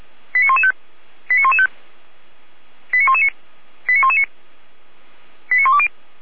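Five-tone selective call (ZVEI), heard as quick runs of five tones, tones of about 70 ms. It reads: 89115 89115 89119 89119 89120
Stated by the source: ZVEI five-tone selective call received on SDR.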